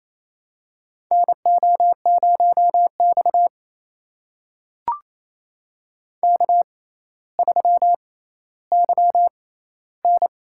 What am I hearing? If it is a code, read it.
Morse code, heard as a keyed tone, about 28 wpm, spaced as for 14 wpm, 700 Hz. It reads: NO0X K3YN
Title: USA Call Signs - 4 Characters - 6x Character Spacing - 28wpm